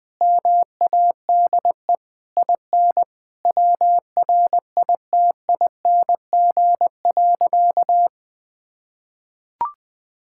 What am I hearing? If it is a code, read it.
Morse code, heard as a keyed tone, about 21 wpm, 700 Hz. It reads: MADE IN WRITING.
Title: Sentences from Top 1600 Words 20wpm